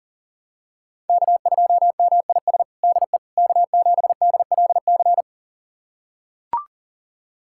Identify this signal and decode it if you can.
Morse code, heard as a keyed tone, about 37 wpm, 700 Hz. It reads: K2MIS DE K7DLC
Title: USA Callsign DE Callsign - 5 Characters - 40wpm